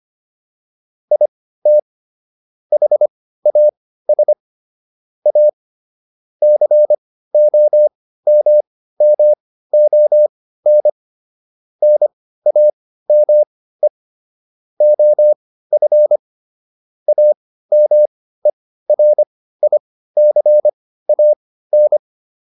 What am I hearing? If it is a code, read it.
Morse code, heard as a keyed tone, about 25 wpm, spaced as for 15 wpm, 600 Hz. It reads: IT HAS A COMMON NAME OF AMERICAN